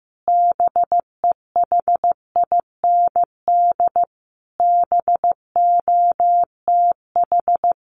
Morse code, heard as a keyed tone, about 15 wpm, 700 Hz. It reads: BEHIND BOTH
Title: Sentences from Top 1400 Words 15wpm